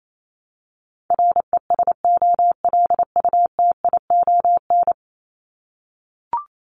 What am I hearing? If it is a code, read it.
Morse code, heard as a keyed tone, about 28 wpm, 700 Hz. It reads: RESOLUTION
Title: Top 1501-1750 Words 28wpm